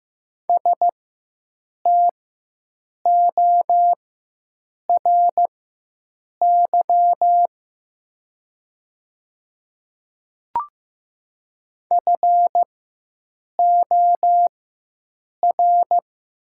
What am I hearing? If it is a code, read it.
Morse code, heard as a keyed tone, about 15 wpm, spaced as for 7 wpm, 700 Hz. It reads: STORY FOR
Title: 4x Character Spacing - Top 300 Words - 15wpm